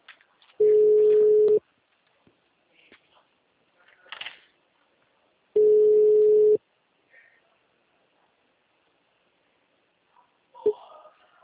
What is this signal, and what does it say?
Ringback tone (Europe, 425 Hz).